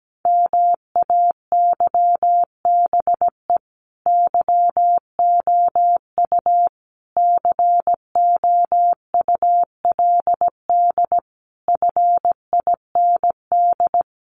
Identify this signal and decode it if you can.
Morse code, heard as a keyed tone, about 17 wpm, 700 Hz. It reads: MAYBE YOU COULD FIND
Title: Sentences from Top 1400 Words 17wpm